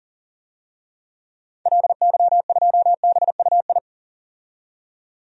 Morse code, heard as a keyed tone, about 40 wpm, 700 Hz. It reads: LY2BUI
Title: DXCC-100 Call Signs - 6 Characters - 40wpm